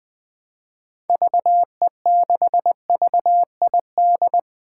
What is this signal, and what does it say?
Morse code, heard as a keyed tone, about 20 wpm, 700 Hz. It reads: VE6VID
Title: 2023 SOTA Chaser Call Signs - 20wpm